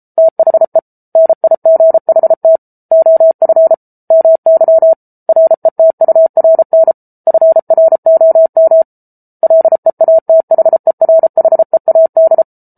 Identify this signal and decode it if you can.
Morse code, heard as a keyed tone, about 34 wpm, 650 Hz. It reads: THE NIGHT OF MY RETURN FROM LEATHERHEAD